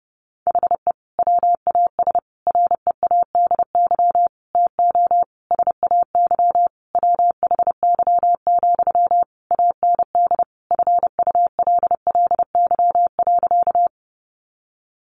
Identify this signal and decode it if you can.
Morse code, heard as a keyed tone, about 30 wpm, 700 Hz. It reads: HE WAS READY TO SAY WHY, AND FULLY.
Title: Sentences from Top 1300 Words 30wpm